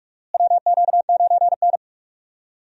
Morse code, heard as a keyed tone, about 42 wpm, 700 Hz. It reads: WQ9N
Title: USA Call Signs - 4 Characters - Repeated 2x - 45wpm